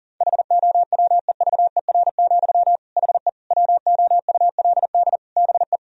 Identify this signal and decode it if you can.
Morse code, heard as a keyed tone, about 40 wpm, 700 Hz. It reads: HOWEVER, HE WOULD BE
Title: Sentences from Top 1000 Words - Review Part 2 - 40wpm